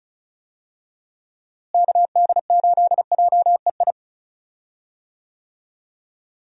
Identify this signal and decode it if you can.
Morse code, heard as a keyed tone, about 34 wpm, 700 Hz. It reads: KD8JEI
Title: USA Call Signs - 6 Characters - Repeated 3x - 35wpm